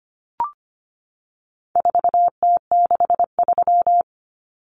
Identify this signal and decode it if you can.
Morse code, heard as a keyed tone, about 25 wpm, 700 Hz. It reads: 4T63
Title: Common Cut Numbers - 4-Digit 25wpm